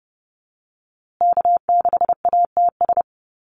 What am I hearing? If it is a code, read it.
Morse code, heard as a keyed tone, about 30 wpm, 700 Hz. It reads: K6ATS
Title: USA Call Signs Repeated 2x 30wpm